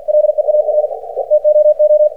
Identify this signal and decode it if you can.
Morse code, heard as a keyed tone, about 50 wpm, 600 Hz. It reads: LEAR TOO
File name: archived / CRSnet